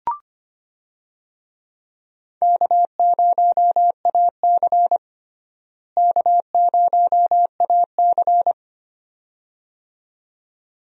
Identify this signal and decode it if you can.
Morse code, heard as a keyed tone, about 25 wpm, 700 Hz. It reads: K0AC K0AC
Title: USA Call Signs - 4 Characters - Repeated 2x - 25wpm